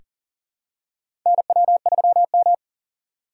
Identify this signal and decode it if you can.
Morse code, heard as a keyed tone, about 40 wpm, 700 Hz. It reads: NW3M
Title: USA Call Signs - 4 Characters - 4x Character Spacing - 40wpm